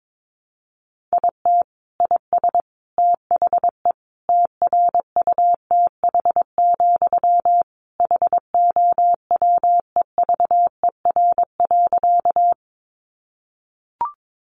Morse code, heard as a keyed tone, about 22 wpm, 700 Hz. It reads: IT IS THE TRUTH, HOWEVER.
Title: Sentences from Top 800 Words 22wpm